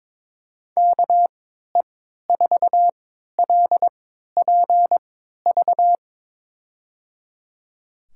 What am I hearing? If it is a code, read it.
Morse code, heard as a keyed tone, about 21 wpm, spaced as for 12 wpm, 700 Hz. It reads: KE4LPV
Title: USA Call Signs - 6 Characters - 3x Character Spacing - 22wpm